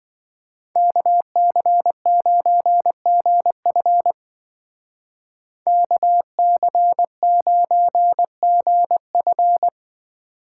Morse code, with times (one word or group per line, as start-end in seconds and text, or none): 0.8-4.1 KC9GF
5.7-9.7 KC9GF